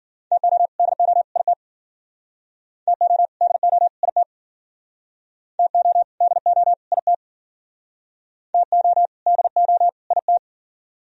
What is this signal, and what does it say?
Morse code, heard as a keyed tone, about 52 wpm, 700 Hz. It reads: TO DO IT TO DO IT TO DO IT TO DO IT